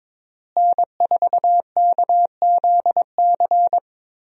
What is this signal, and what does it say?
Morse code, heard as a keyed tone, about 22 wpm, 700 Hz. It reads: N4KZC
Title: USA Call Signs - 5 Characters - Repeated 3x - 22wpm